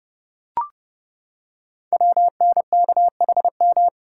Morse code, heard as a keyed tone, about 30 wpm, 700 Hz. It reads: WNKHM
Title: Code Groups - 5 Letters - 30wpm